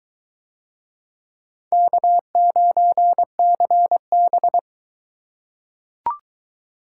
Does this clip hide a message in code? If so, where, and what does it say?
Yes: Morse code from 1.7 to 4.6 s, heard as a keyed tone, about 22 wpm, 700 Hz. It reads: K9CB